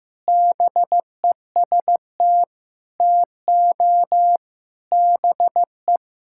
Morse code, heard as a keyed tone, about 15 wpm, 700 Hz. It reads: BEST TO BE
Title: Sentences from Top 1000 Words - Review Part 6 - 15wpm